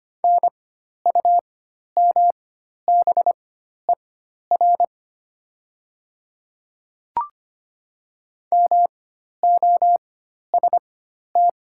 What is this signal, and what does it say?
Morse code, heard as a keyed tone, about 25 wpm, spaced as for 12 wpm, 690 Hz. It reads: NUMBER MOST
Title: 4x Character Spacing - Top 200 Words - 25wpm